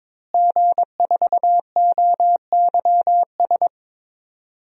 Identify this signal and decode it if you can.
Morse code, heard as a keyed tone, about 22 wpm, 700 Hz. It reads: G4OYS
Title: DXCC-100 Call Signs - 5 Characters - Repeated 2x - 22wpm